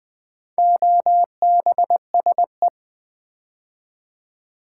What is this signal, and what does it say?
Morse code, heard as a keyed tone, about 20 wpm, 700 Hz. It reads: OBSE
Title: Build A Word - 4 Letters At A Time - 8-Letter Words - 20wpm